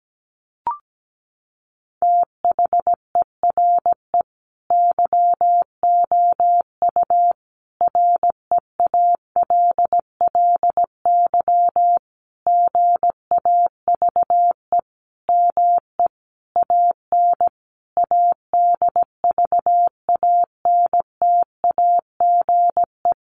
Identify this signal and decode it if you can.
Morse code, heard as a keyed tone, about 17 wpm, 700 Hz. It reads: THERE YOU REALLY GAVE ME AN ADVANTAGE
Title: Sentences from Top 900 Words 17wpm